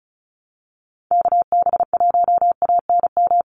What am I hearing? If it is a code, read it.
Morse code, heard as a keyed tone, about 34 wpm, 700 Hz. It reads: KB1ANM
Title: USA Call Signs 35wpm